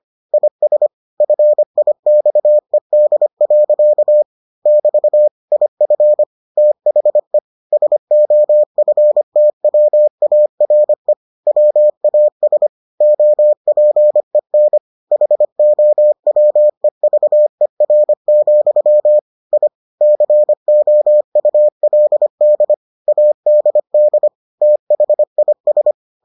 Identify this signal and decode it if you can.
Morse code, heard as a keyed tone, about 25 wpm, 600 Hz. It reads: IS FIXED. <BT> IF THE SOFTWARE WAS OPEN HOWEVER, I COULD ADD THIS